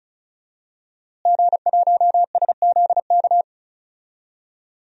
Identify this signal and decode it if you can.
Morse code, heard as a keyed tone, about 36 wpm, 700 Hz. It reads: G1SZK
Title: DXCC-100 Call Signs - 5 Characters - Repeated 2x - 35wpm